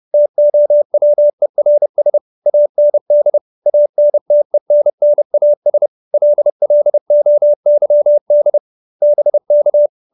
Morse code, heard as a keyed tone, about 30 wpm, 590 Hz. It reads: TOWERS AND ANTENNAS LLOYD BK
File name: QSO311 30